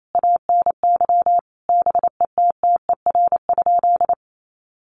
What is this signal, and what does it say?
Morse code, heard as a keyed tone, about 28 wpm, 700 Hz. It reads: ANY BETTER?
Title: Sentences from Top 1200 Words 28wpm